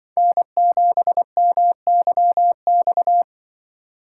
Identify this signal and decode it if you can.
Morse code, heard as a keyed tone, about 24 wpm, 700 Hz. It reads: N7MYX
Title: Speed Racing - USA 1x3 Call Signs 20wpm